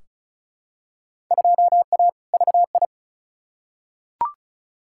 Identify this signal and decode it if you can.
Morse code, heard as a keyed tone, about 36 wpm, 700 Hz. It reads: 2A VI
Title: Field Day Exchanges - 35wpm